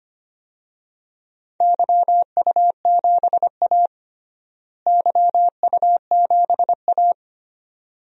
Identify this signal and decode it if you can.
Morse code, heard as a keyed tone, about 25 wpm, 700 Hz. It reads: YU7A YU7A
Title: DXCC-100 Call Signs - 4 Characters - Repeated 2x - 25wpm